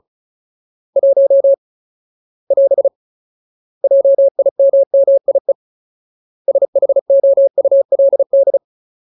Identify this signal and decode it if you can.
Morse code, heard as a keyed tone, about 35 wpm, 550 Hz. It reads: 1 <AS> JIMMIE SHOULD
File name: Ultimate Word List / 35WPM Ultimate